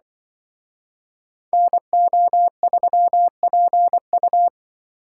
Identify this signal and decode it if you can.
Morse code, heard as a keyed tone, about 24 wpm, 700 Hz. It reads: NO3PU
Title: Speed Racing - USA 2x2 Call Signs 20wpm